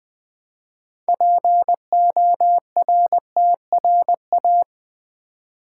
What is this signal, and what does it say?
Morse code, heard as a keyed tone, about 20 wpm, 700 Hz. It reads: PORTRA
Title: Build A Word - 3 Letters At A Time - 8-Letter Words - 20wpm